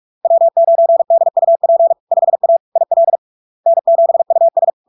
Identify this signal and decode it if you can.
Morse code, heard as a keyed tone, about 46 wpm, 670 Hz. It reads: W9DUP 5A IL N7US
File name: LARK-FD-2022-Log-in-CW-45wpm-670-Hz